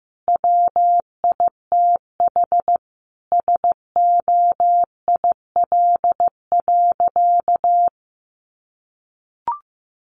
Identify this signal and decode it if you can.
Morse code, heard as a keyed tone, about 15 wpm, 700 Hz. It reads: WITH SOIL.
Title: Easy Phrases 15wpm